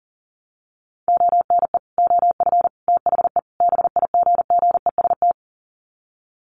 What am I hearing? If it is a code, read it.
Morse code, heard as a keyed tone, about 40 wpm, 700 Hz. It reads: ONE OF THE BIGGEST
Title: Sets of 4 Words 40wpm